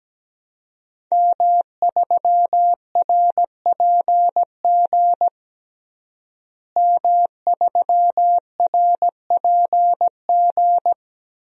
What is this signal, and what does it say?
Morse code, heard as a keyed tone, about 17 wpm, 700 Hz. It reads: M3RPG M3RPG
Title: DXCC-100 Call Signs - 5 Characters - Repeated 2x - 17wpm